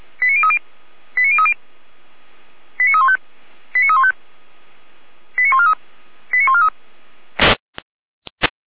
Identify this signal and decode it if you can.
Five-tone selective call (ZVEI), heard as quick runs of five tones, tones of about 70 ms. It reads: 89030 89030 89315 89315 89143 89143